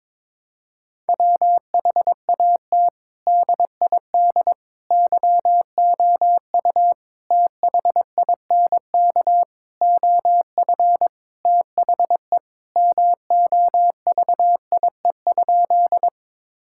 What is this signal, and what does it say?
Morse code, heard as a keyed tone, about 21 wpm, 700 Hz. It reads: WHAT DID YOU THINK OF THE MOVIE?
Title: Sentences from Top 1000 Words - Review Part 5 - 22wpm